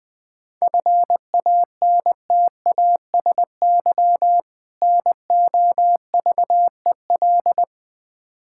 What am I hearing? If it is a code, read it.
Morse code, heard as a keyed tone, about 21 wpm, 700 Hz. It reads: FANTASY NOVEL
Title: Difficult Words with Hints - 20wpm